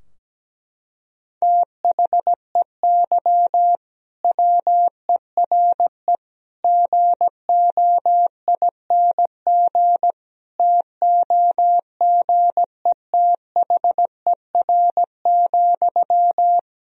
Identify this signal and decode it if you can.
Morse code, heard as a keyed tone, about 17 wpm, 700 Hz. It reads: THEY WERE GOING TOGETHER,
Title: Sentences from Top 1000 Words - Review Part 4 - 17wpm